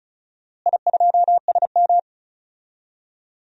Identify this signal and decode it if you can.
Morse code, heard as a keyed tone, about 36 wpm, 700 Hz. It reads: I2SM
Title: DXCC-100 Call Signs - 4 Characters - Repeated 3x - 35wpm